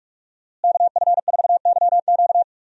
Morse code, heard as a keyed tone, about 46 wpm, 700 Hz. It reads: KF4YQ
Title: USA Call Signs - 5 Characters - Repeated 2x - 45wpm